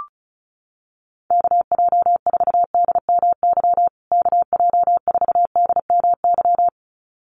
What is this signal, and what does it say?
Morse code, heard as a keyed tone, about 34 wpm, 700 Hz. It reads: KJ4DMY KJ4DMY
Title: USA Call Signs Repeated 2x 35wpm